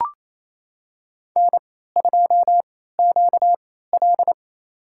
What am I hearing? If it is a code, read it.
Morse code, heard as a keyed tone, about 28 wpm, spaced as for 16 wpm, 700 Hz. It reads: N2QL
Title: USA Call Signs - 4 Characters - 3x Character Spacing - 28wpm